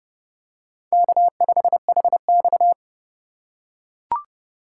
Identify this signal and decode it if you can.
Morse code, heard as a keyed tone, about 28 wpm, 700 Hz. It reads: K5HX